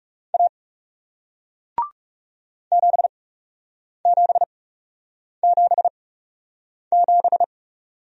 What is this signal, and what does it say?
Morse code, heard as a keyed tone, about 40 wpm, spaced as for 8 wpm, 700 Hz. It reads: A 7777